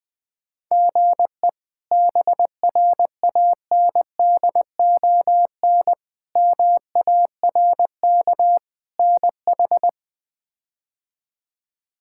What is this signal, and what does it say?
Morse code, heard as a keyed tone, about 20 wpm, 700 Hz. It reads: GE BRANDON MARK NH
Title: K1USN SST QSO Practice - 20wpm